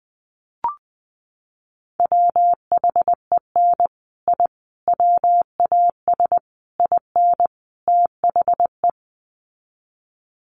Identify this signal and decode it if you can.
Morse code, heard as a keyed tone, about 20 wpm, 700 Hz. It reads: WHEN I WAS IN THE
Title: Sets of 5 Words Encore2 20wpm